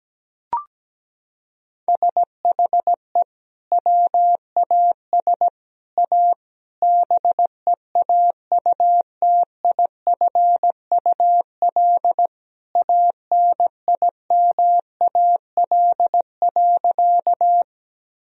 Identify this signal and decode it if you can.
Morse code, heard as a keyed tone, about 17 wpm, 700 Hz. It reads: SHE WAS A BEAUTIFUL ANIMAL.
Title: Sentences from Top 1000 Words - Review Part 1 - 17wpm